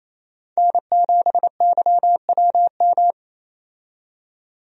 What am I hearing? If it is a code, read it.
Morse code, heard as a keyed tone, about 28 wpm, 700 Hz. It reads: N7YWM
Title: USA Call Signs - 5 Characters - 28wpm